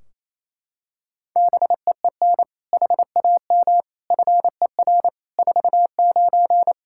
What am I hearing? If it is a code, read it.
Morse code, heard as a keyed tone, about 28 wpm, 700 Hz. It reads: BEEN HAM FER 49
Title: Practice QSO - Exchange 3 - More Info - 28wpm